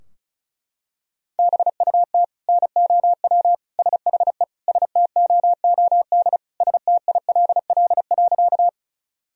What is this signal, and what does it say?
Morse code, heard as a keyed tone, about 36 wpm, 700 Hz. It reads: BUT NOW SHE STOOD STILL.
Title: Sentences from Top 1000 Words - Review Part 2 - 35wpm